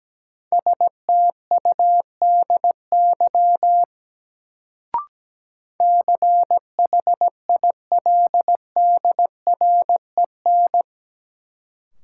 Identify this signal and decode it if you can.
Morse code, heard as a keyed tone, about 17 wpm, 700 Hz. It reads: STUDY CHILDREN
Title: Sentences from Top 300 Words - New Word Drill - 17wpm